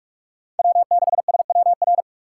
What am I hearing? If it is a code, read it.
Morse code, heard as a keyed tone, about 46 wpm, 690 Hz. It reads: W6SWR
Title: USA Call Signs - 5 Characters - 6x Character Spacing - 45wpm